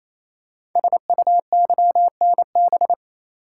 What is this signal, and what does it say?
Morse code, heard as a keyed tone, about 27 wpm, 700 Hz. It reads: SUYNB
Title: Code Groups - 5 Letters - 28wpm